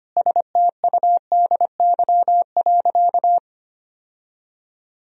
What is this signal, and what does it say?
Morse code, heard as a keyed tone, about 25 wpm, 700 Hz. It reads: STUDY.
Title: Sentences from Top 1000 Words - Review Part 1 - 25wpm